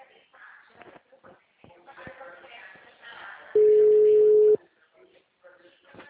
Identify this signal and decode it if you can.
Ringback tone (Europe, 425 Hz), with speech.